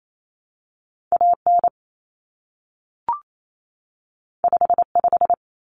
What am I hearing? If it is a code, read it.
Morse code, heard as a keyed tone, about 28 wpm, 700 Hz. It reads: AN 55
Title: Common Cut Numbers - 2-Digit 28wpm